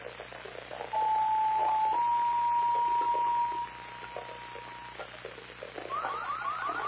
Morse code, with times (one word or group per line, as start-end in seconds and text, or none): none